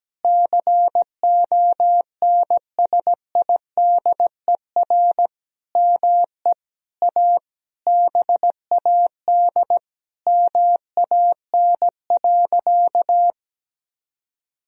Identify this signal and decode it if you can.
Morse code, heard as a keyed tone, about 17 wpm, 700 Hz. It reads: CONSIDER ME A BAD MAN.
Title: Sentences from Top 1000 Words - Review Part 4 - 17wpm